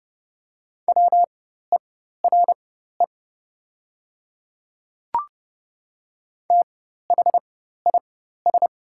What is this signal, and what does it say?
Morse code, heard as a keyed tone, about 30 wpm, spaced as for 14 wpm, 700 Hz. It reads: WERE THIS
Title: 4x Character Spacing - Top 100 Words - 30wpm